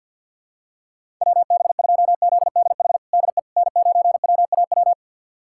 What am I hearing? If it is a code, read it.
Morse code, heard as a keyed tone, about 46 wpm, 700 Hz. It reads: WB2ZDH DE N9WAW
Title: USA Callsign DE Callsign 50wpm